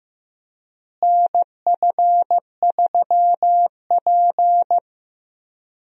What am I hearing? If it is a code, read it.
Morse code, heard as a keyed tone, about 15 wpm, 700 Hz. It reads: NF3P
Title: USA Call Signs - 4 Characters - Repeated 2x - 15wpm